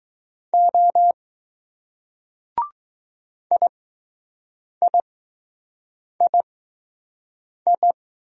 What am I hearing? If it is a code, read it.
Morse code, heard as a keyed tone, about 20 wpm, spaced as for 6 wpm, 700 Hz. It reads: O IIII